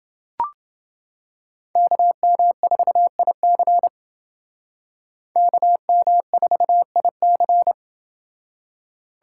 Morse code, heard as a keyed tone, about 29 wpm, 700 Hz. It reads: KM4IC KM4IC